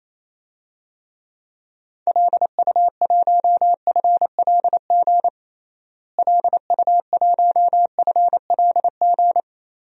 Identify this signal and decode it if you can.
Morse code, heard as a keyed tone, about 28 wpm, 700 Hz. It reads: LU1FLG LU1FLG
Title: DXCC-100 Call Signs - 6 Characters - Repeated 3x - 28wpm